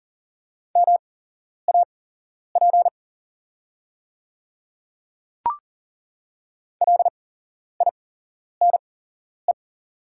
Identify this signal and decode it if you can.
Morse code, heard as a keyed tone, about 41 wpm, spaced as for 11 wpm, 700 Hz. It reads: MAP LINE
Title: 8x Character Spacing - Top 500 Words - 40wpm